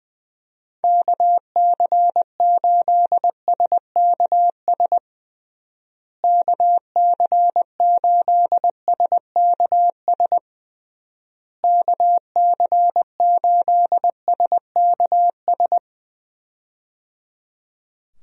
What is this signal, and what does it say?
Morse code, heard as a keyed tone, about 20 wpm, 700 Hz. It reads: KC8SKS KC8SKS KC8SKS
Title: USA Call Signs - 6 Characters - Repeated 3x - 20wpm